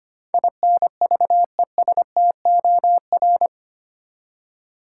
Morse code, heard as a keyed tone, about 25 wpm, 700 Hz. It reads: INVESTOR
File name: Eight Letter Words - 25wpm